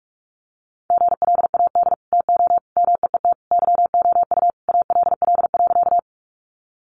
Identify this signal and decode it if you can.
Morse code, heard as a keyed tone, about 44 wpm, 700 Hz. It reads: GLAD TO MEET YOU ALL.
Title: Sentences from Top 1100 Words 45wpm